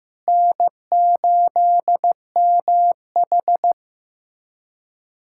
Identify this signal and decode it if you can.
Morse code, heard as a keyed tone, about 15 wpm, 700 Hz. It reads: N8MH
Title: USA Callsign DE Callsign - 4 Characters - 15wpm